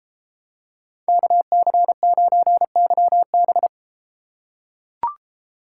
Morse code, heard as a keyed tone, about 31 wpm, 700 Hz. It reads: KC9YB